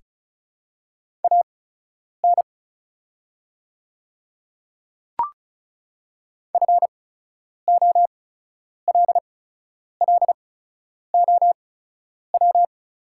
Morse code, heard as a keyed tone, about 36 wpm, spaced as for 10 wpm, 700 Hz. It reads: AN FOLLOW
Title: 8x Character Spacing - Top 200 Words - 35wpm